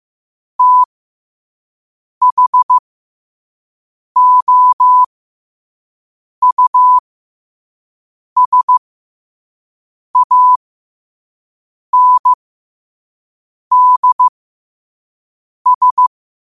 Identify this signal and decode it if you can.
Morse code, heard as a keyed tone, about 15 wpm, spaced as for 5 wpm, 1000 Hz. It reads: THOUSANDS
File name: qotdfiles / 5wpm